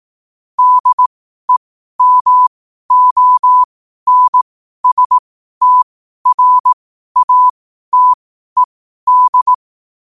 Morse code, heard as a keyed tone, about 18 wpm, spaced as for 12 wpm, 1000 Hz. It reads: DEMONSTRATED